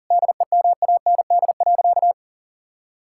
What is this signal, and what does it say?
Morse code, heard as a keyed tone, about 40 wpm, 700 Hz. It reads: DEMAND.
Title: Sentences from Top 1000 Words - Review Part 5 - 40wpm